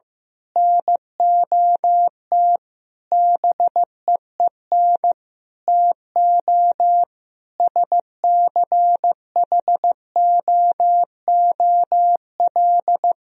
Morse code, heard as a keyed tone, about 15 wpm, 700 Hz. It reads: NOT BEEN TO SCHOOL
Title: Sentences from Top 1000 Words - Review Part 1 - 15wpm